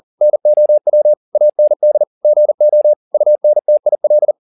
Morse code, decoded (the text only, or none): NOW AND GO UNTIL